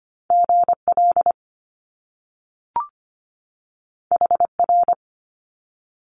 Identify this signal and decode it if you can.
Morse code, heard as a keyed tone, about 25 wpm, 700 Hz. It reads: GL HR
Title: Top Two Letter Combinations in Words 25wpm